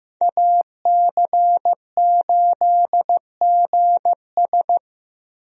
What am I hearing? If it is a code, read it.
Morse code, heard as a keyed tone, about 15 wpm, 700 Hz. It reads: AC8GS
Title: USA Call Signs - 5 Characters - Repeated 2x - 15wpm